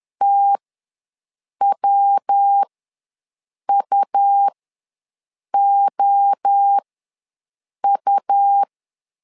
Morse code, heard as a keyed tone, about 10 wpm, spaced as for 6 wpm, 800 Hz. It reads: TWUOU
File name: LEZIONI / 3 MWRO / 3 MWRO ws 10wpm 2310